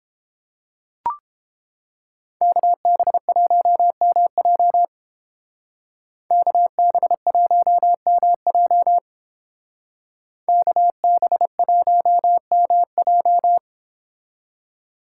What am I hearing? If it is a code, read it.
Morse code, heard as a keyed tone, about 28 wpm, 700 Hz. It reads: KB1MJ KB1MJ KB1MJ